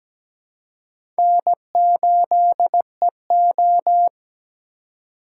Morse code, heard as a keyed tone, about 17 wpm, 700 Hz. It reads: N8EO
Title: USA Call Signs - 4 Characters - 3x Character Spacing - 17wpm